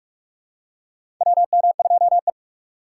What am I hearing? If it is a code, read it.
Morse code, heard as a keyed tone, about 44 wpm, 690 Hz. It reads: WM2E
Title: USA Call Signs - 3x Character Spacing - 45wpm